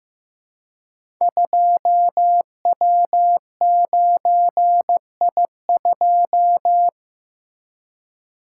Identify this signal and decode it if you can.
Morse code, heard as a keyed tone, about 15 wpm, 700 Hz. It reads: 2W9I2
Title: Code Groups - 5 Letter:Numbers - 15wpm